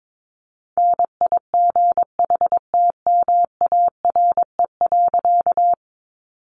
Morse code, heard as a keyed tone, about 22 wpm, 700 Hz. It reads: NIGHTMARE.
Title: A Sound Of Thunder 22wpm